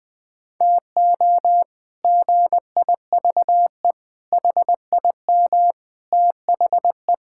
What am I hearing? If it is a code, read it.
Morse code, heard as a keyed tone, about 20 wpm, 700 Hz. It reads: TO GIVE HIM THE
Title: Sentences with 8-Word Phrases - 20wpm